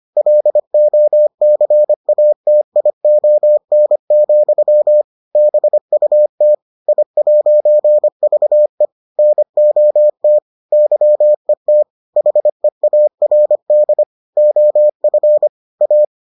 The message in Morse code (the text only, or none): LOCATION, BUT I'VE NOT YET HEARD OF A